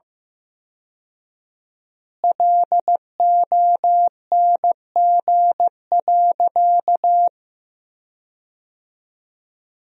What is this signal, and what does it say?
Morse code, heard as a keyed tone, about 15 wpm, 700 Hz. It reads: LONG.
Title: 4x Word-Spacing - Sentences from Top 100 Words - 15wpm